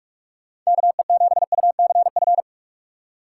Morse code, heard as a keyed tone, about 45 wpm, 690 Hz. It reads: KE7UCF
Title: USA Call Signs - 6 Characters - 2x Character Spacing - 45wpm